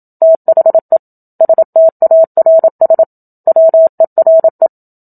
Morse code, heard as a keyed tone, about 27 wpm, 650 Hz. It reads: THE STARS WERE
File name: Book Two / 01-Chapter 1 - 27 WPM